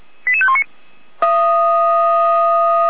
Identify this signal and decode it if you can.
Five-tone selective call (ZVEI), heard as quick runs of five tones, tones of about 70 ms. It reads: 88519